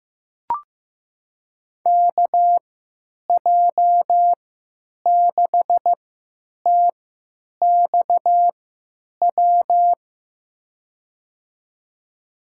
Morse code, heard as a keyed tone, about 15 wpm, spaced as for 9 wpm, 700 Hz. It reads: KJ6TXW
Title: USA Call Signs - 6 Characters - 3x Character Spacing - 15wpm